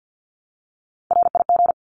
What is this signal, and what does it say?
Morse code, heard as a keyed tone, about 50 wpm, 700 Hz. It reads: RIG
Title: Abbreviations Repeated 3x - 50wpm